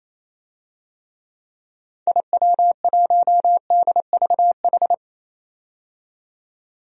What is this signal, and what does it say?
Morse code, heard as a keyed tone, about 28 wpm, 700 Hz. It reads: IW1DVH
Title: DXCC-100 Call Signs - 6 Characters - 28wpm